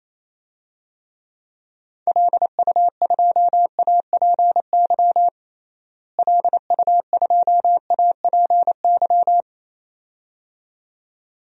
Morse code, heard as a keyed tone, about 28 wpm, 700 Hz. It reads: LU2APY LU2APY
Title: DXCC-100 Call Signs - 6 Characters - Repeated 2x - 28wpm